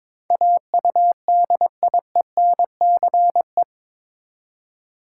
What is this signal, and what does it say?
Morse code, heard as a keyed tone, about 21 wpm, 700 Hz. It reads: AUDIENCE
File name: practice / 22wpm / Eight Letter Words - 22wpm